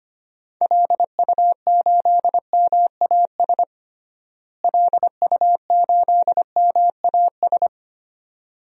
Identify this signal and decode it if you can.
Morse code, heard as a keyed tone, about 25 wpm, 700 Hz. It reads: LU8MAS LU8MAS
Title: DXCC-100 Call Signs - 6 Characters - Repeated 2x - 25wpm